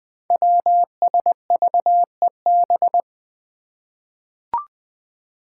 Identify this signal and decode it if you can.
Morse code, heard as a keyed tone, about 20 wpm, 700 Hz. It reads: WSVEB
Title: Five Challenging Letter-Number Combinations 20wpm